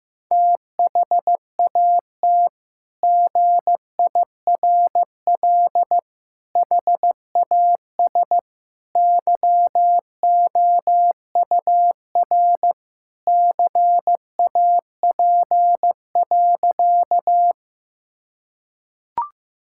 Morse code, heard as a keyed tone, about 15 wpm, 700 Hz. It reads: THAT GIRL HAS YOUR CAP.
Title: Sentences from Top 1600 Words 15wpm